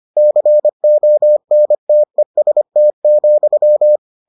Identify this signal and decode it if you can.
Morse code, heard as a keyed tone, about 25 wpm, 600 Hz. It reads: CONTEST,